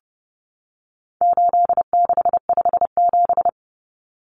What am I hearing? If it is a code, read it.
Morse code, heard as a keyed tone, about 30 wpm, 700 Hz. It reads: 8657
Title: Common Cut Numbers - 4-Digit 30wpm